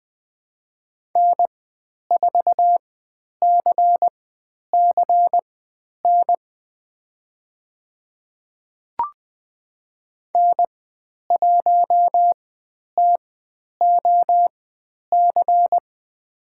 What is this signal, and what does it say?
Morse code, heard as a keyed tone, about 20 wpm, spaced as for 10 wpm, 700 Hz. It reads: N4CCN N1TOC
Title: 102 Lesson 17g - Letter u - Call Signs 20wpm